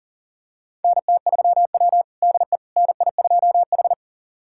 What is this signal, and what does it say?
Morse code, heard as a keyed tone, about 40 wpm, 700 Hz. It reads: NT3W DE NI2H
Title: USA Callsign DE Callsign - 4 Characters - 40wpm